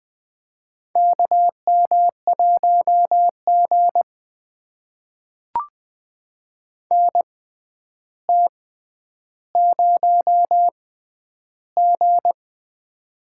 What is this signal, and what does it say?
Morse code, heard as a keyed tone, about 20 wpm, spaced as for 10 wpm, 700 Hz. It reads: KM1G NT0G
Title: USA Call Signs - 4 Characters - 6x Character Spacing - 20wpm